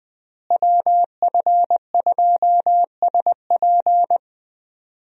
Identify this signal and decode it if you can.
Morse code, heard as a keyed tone, about 20 wpm, 700 Hz. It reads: WF2SP
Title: USA Call Signs - 5 Characters - Repeated 3x - 20wpm